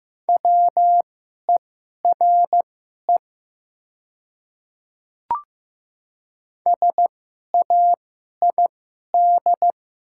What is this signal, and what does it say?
Morse code, heard as a keyed tone, about 15 wpm, spaced as for 11 wpm, 700 Hz. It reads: WERE SAID
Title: 2x Character Spacing - Top 100 Words - 15wpm